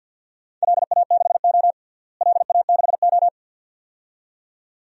Morse code, heard as a keyed tone, about 52 wpm, 700 Hz. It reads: LA6O LA6O
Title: DXCC-100 Call Signs - 4 Characters - Repeated 2x - 50wpm